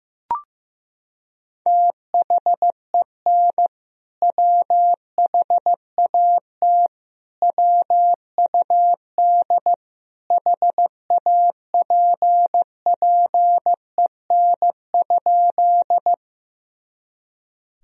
Morse code, heard as a keyed tone, about 15 wpm, 700 Hz. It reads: THEN WHAT WUD HAPPEN?
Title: Sentences from Top 400 Words and Abbreviations 15wpm